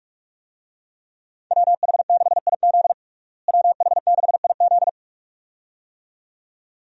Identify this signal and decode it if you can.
Morse code, heard as a keyed tone, about 47 wpm, 700 Hz. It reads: WH6IZ WH6IZ
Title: USA Call Signs - 5 Characters - Repeated 2x - 45wpm